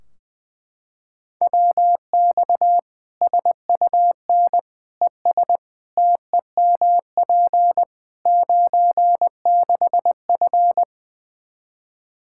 Morse code, heard as a keyed tone, about 20 wpm, 700 Hz. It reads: WX SUN ES TEMP 96F
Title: Practice QSO - Exchange 2 - Info - 20wpm